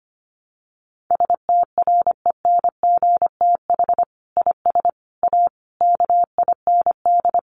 Morse code, heard as a keyed tone, about 25 wpm, 700 Hz. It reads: STRENGTH IS A KIND
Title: Sentences from Top 1300 Words 25wpm